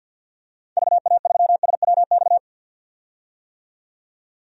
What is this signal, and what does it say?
Morse code, heard as a keyed tone, about 49 wpm, 700 Hz. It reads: VA3SWX
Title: DXCC-100 Call Signs - 6 Characters - 50wpm